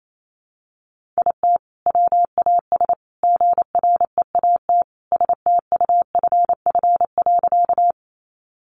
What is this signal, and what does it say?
Morse code, heard as a keyed tone, about 28 wpm, 700 Hz. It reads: IT WAS GREAT STUFF.
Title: Sentences from Top 1400 Words 28wpm